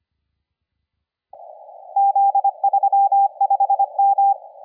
Morse code, heard as a keyed tone, about 25 wpm, 790 Hz. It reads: Z35M
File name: la4c / sac-cw-2017 / CONTEST AUDIO 1100Z